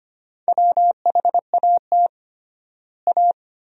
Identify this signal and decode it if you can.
Morse code, heard as a keyed tone, about 25 wpm, 700 Hz. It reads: WHAT A
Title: 3x Word-Spacing - Sentences from Top 300 Words - 25wpm